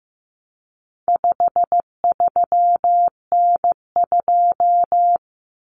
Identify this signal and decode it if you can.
Morse code, heard as a keyed tone, about 15 wpm, 700 Hz. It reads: 53N2
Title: Common Cut Numbers - 4-Digit 15wpm